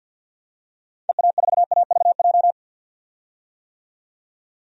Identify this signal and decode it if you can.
Morse code, heard as a keyed tone, about 49 wpm, 700 Hz. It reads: EA4AVJ
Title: DXCC-100 Call Signs - 6 Characters - 50wpm